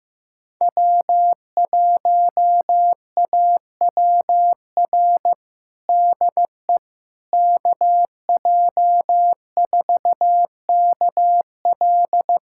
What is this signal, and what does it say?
Morse code, heard as a keyed tone, about 15 wpm, 700 Hz. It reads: W1AWR DE KJ4KL
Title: USA Callsign DE Callsign - 5 Characters - 15wpm